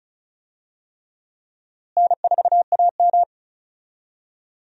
Morse code, heard as a keyed tone, about 34 wpm, 700 Hz. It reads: N4AM
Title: USA Call Signs - 4 Characters - Repeated 3x - 35wpm